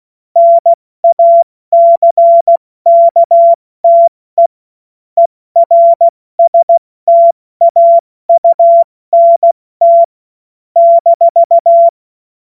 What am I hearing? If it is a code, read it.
Morse code, heard as a keyed tone, about 16 wpm, 680 Hz. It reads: NACKTE ERSTAUNT -
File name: Morsekurs / Witze / Witz 18. Januar 2022, 70 F80